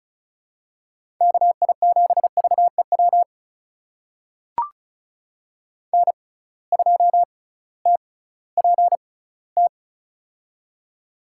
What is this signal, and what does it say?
Morse code, heard as a keyed tone, about 34 wpm, spaced as for 19 wpm, 700 Hz. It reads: KI7VEW N2TPT
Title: USA Call Signs - 6x Character Spacing - 35wpm